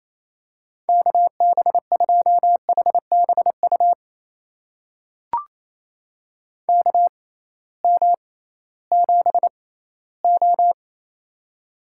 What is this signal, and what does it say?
Morse code, heard as a keyed tone, about 28 wpm, spaced as for 16 wpm, 700 Hz. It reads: KB2HBU KM7O
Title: USA Call Signs - 6x Character Spacing - 28wpm